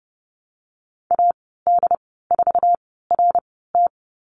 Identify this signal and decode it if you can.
Morse code, heard as a keyed tone, about 32 wpm, spaced as for 17 wpm, 700 Hz. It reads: AD4RT
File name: Lesson 14g - Number 2 - Call Signs 30wpm